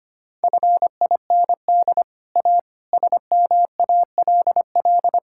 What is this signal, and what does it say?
Morse code, heard as a keyed tone, about 25 wpm, 700 Hz. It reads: FIND A SMALL
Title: Sentences from Top 1000 Words - Review Part 5 - 25wpm